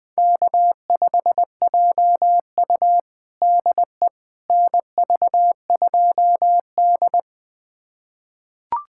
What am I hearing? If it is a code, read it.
Morse code, heard as a keyed tone, about 20 wpm, 700 Hz. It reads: K5JU DE NV2D
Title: USA Callsign DE Callsign - 4 Characters - 20wpm